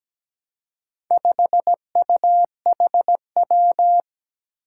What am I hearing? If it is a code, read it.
Morse code, heard as a keyed tone, about 17 wpm, 700 Hz. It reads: 5UHW
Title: Four Challenging Letter-Number Combinations 17wpm